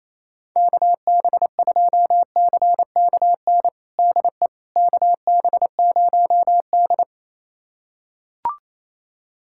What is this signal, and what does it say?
Morse code, heard as a keyed tone, about 28 wpm, 700 Hz. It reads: KB2CKN DE KB0D
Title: USA Callsign DE Callsign 28wpm